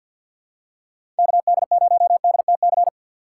Morse code, heard as a keyed tone, about 48 wpm, 700 Hz. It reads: KD0DTC
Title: USA Call Signs - 6 Characters - 4x Character Spacing - 50wpm